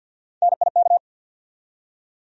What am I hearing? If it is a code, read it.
Morse code, heard as a keyed tone, about 48 wpm, 700 Hz. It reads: NIK
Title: USA Call Sign Postfixes - 3 Letters 50wpm